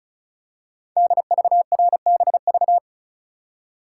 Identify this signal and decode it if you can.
Morse code, heard as a keyed tone, about 36 wpm, 700 Hz. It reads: DVRBV
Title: Code Groups - 5 Letters - 35wpm